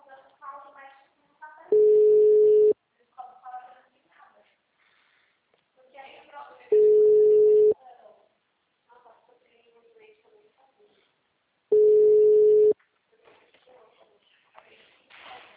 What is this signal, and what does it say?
Ringback tone (Europe, 425 Hz), with speech.
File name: out- -2016- - - .243